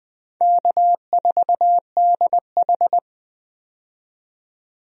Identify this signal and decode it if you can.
Morse code, heard as a keyed tone, about 20 wpm, 700 Hz. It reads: K4DH
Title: USA Call Signs - 4 Characters - 20wpm